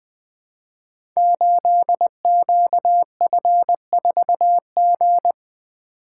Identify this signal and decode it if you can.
Morse code, heard as a keyed tone, about 20 wpm, 700 Hz. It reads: 8QF4G
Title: Code Groups - 5 Letter:Numbers - 20wpm